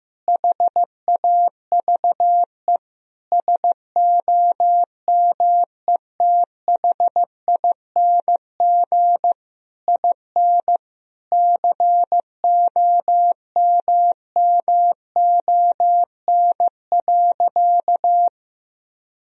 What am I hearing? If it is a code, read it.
Morse code, heard as a keyed tone, about 15 wpm, 700 Hz. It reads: HAVE SOMETHING IN COMMON.
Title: Sentences from Top 1000 Words - Review Part 3 - 15wpm